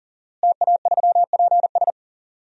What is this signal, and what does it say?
Morse code, heard as a keyed tone, about 40 wpm, 700 Hz. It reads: TA3PS
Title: DXCC-100 Call Signs - 5 Characters - Repeated 3x - 40wpm